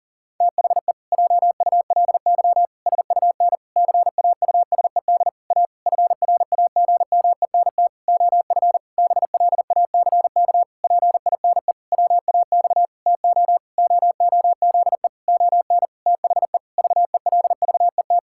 Morse code, heard as a keyed tone, about 40 wpm, 700 Hz. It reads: THE JULY SUN CAUSED A FRAGMENT OF BLACK PINE WAX TO OOZE ON THE VELVET
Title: Pangram Challenge v2 - 40wpm